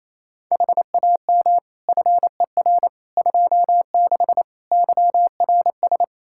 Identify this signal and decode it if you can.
Morse code, heard as a keyed tone, about 27 wpm, 700 Hz. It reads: HAM FER 26 YRS
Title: Practice QSO - Exchange 3 - More Info - 28wpm